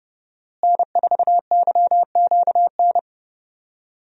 Morse code, heard as a keyed tone, about 30 wpm, 700 Hz. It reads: N4YQN
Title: USA Call Signs - 5 Characters - Repeated 3x - 30wpm